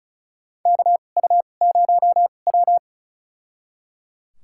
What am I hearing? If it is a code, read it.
Morse code, heard as a keyed tone, about 36 wpm, spaced as for 25 wpm, 700 Hz. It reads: KU0W
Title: USA Call Signs - 4 Characters - 2x Character Spacing - 35wpm